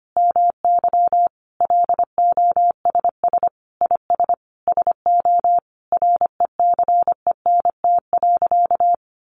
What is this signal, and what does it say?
Morse code, heard as a keyed tone, about 25 wpm, 700 Hz. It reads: MY LOSS IS SO RECENT.
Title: Sentences from Top 1400 Words 25wpm